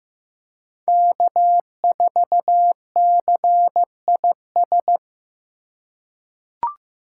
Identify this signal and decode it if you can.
Morse code, heard as a keyed tone, about 15 wpm, 700 Hz. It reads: K4CIS
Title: USA Call Signs - 2x Character Spacing - 15wpm